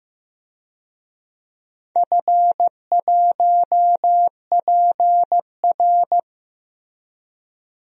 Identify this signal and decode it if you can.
Morse code, heard as a keyed tone, about 15 wpm, 700 Hz. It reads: F1PR
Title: DXCC-100 Call Signs - 4 Characters - Repeated 3x - 15wpm